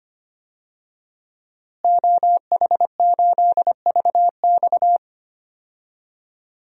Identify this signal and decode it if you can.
Morse code, heard as a keyed tone, about 25 wpm, 700 Hz. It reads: OH8VX
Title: DXCC-100 Call Signs - 5 Characters - Repeated 2x - 25wpm